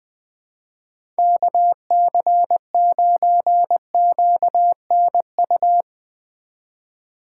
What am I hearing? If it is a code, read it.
Morse code, heard as a keyed tone, about 21 wpm, 700 Hz. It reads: KC9QNU
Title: USA Call Signs - 4x Character Spacing - 20wpm